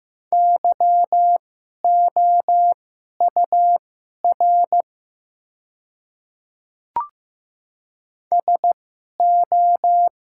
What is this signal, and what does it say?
Morse code, heard as a keyed tone, about 15 wpm, spaced as for 11 wpm, 700 Hz. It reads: YOUR SO
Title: 2x Character Spacing - Top 100 Words - 15wpm